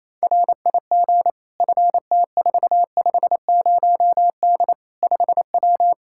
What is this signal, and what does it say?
Morse code, heard as a keyed tone, about 28 wpm, 700 Hz. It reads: RIG FT450D 5W
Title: Practice QSO - Exchange 2 - Info - 28wpm